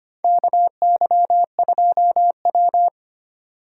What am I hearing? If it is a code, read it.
Morse code, heard as a keyed tone, about 25 wpm, 700 Hz. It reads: KY2W
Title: USA Call Signs - 4 Characters - 2x Character Spacing - 25wpm